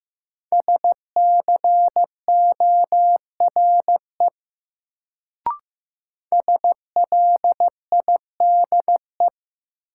Morse code, heard as a keyed tone, about 15 wpm, 700 Hz. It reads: SCORE SLIDE
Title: Sentences from Top 1600 Words - New Word Drill - 15wpm